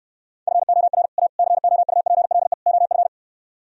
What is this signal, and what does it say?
Morse code, heard as a keyed tone, about 67 wpm, 700 Hz. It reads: FOR A COUPLE OF